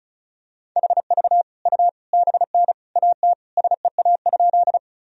Morse code, heard as a keyed tone, about 36 wpm, 700 Hz. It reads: HV U BN AT SEA?
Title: Sentences from Top 300 Words and Abbreviations 35wpm